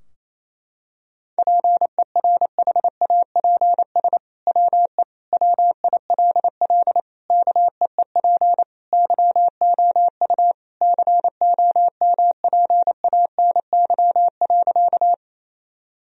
Morse code, heard as a keyed tone, about 28 wpm, 700 Hz. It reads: PERHAPS WE WILL KEEP YOU COMPANY.
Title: Sentences from Top 1000 Words - Review Part 3 - 28wpm